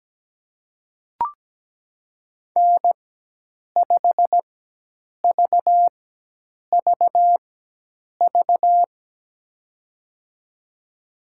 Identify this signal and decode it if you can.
Morse code, heard as a keyed tone, about 17 wpm, spaced as for 8 wpm, 700 Hz. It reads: N5VVV
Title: USA Call Signs - 5 Characters - 4x Character Spacing - 17wpm